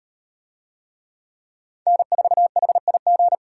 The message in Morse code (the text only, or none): N4HIG